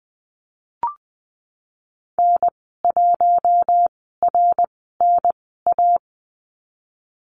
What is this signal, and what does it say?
Morse code, heard as a keyed tone, about 20 wpm, spaced as for 15 wpm, 700 Hz. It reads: N1RNA
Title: 062 Lesson 11g - Letter h - Call Signs 20wpm